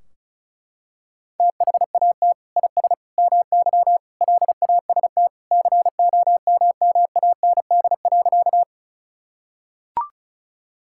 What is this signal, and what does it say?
Morse code, heard as a keyed tone, about 34 wpm, 700 Hz. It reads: THAT IS MY LAST COMMAND.
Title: Sentences from Top 1600 Words 35wpm